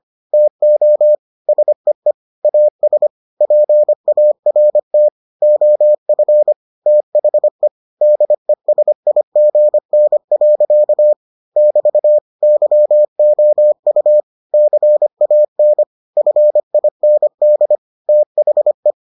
Morse code, heard as a keyed tone, about 25 wpm, 600 Hz. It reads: TO SEE AS PART OF THE DESIGN. <BT> YOU CAN FIND THE